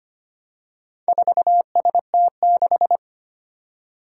Morse code, heard as a keyed tone, about 25 wpm, 700 Hz. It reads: 4ST6
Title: Four Challenging Letter-Number Combinations 25wpm